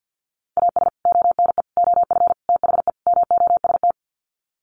Morse code, heard as a keyed tone, about 50 wpm, 700 Hz. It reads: AS ONE OF THE MOST
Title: Sets of 5 Words 50wpm